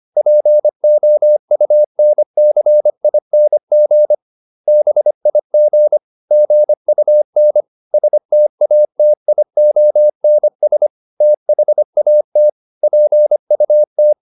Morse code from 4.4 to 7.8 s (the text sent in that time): BIG GUN